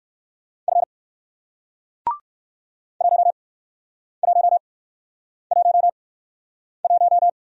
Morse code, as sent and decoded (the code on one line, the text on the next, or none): ..-. / .---- .---- .---- .----
F 1111